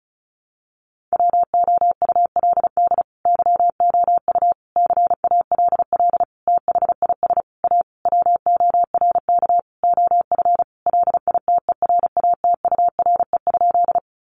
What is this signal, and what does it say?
Morse code, heard as a keyed tone, about 34 wpm, 700 Hz. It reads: WOULD YOU CALL THIS A WORK OF LITERATURE?
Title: Sentences from Top 600 Words 35wpm